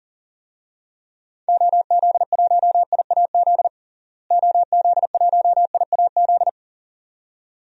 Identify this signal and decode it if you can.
Morse code, heard as a keyed tone, about 37 wpm, 700 Hz. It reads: OZ1IAZ OZ1IAZ
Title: DXCC-100 Call Signs - 6 Characters - Repeated 2x - 40wpm